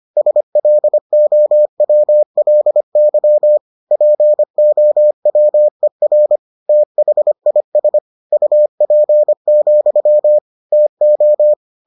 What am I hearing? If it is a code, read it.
Morse code, heard as a keyed tone, about 25 wpm, 600 Hz. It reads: SLOWLY POWER THIS UP, TO